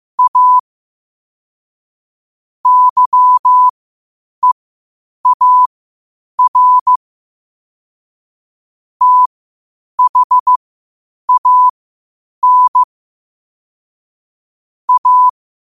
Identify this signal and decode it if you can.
Morse code, heard as a keyed tone, about 15 wpm, spaced as for 8 wpm, 1000 Hz. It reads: A YEAR THAN A
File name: qotdfiles / 8wpm